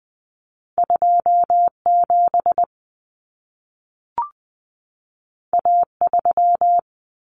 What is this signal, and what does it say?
Morse code, heard as a keyed tone, about 20 wpm, 700 Hz. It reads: 27 A3
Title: Common Cut Numbers - 2-Digit 20wpm